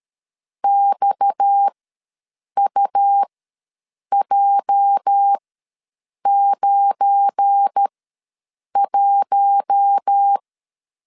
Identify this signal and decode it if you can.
Morse code, heard as a keyed tone, about 13 wpm, spaced as for 7 wpm, 800 Hz. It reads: XUJ91
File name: LEZIONI / 8 YJ45 / 8 YJ ws 12wpm 3342